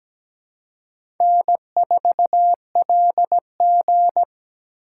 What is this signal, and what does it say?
Morse code, heard as a keyed tone, about 17 wpm, 700 Hz. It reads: N4LG
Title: USA Call Signs - 4 Characters - 4x Character Spacing - 17wpm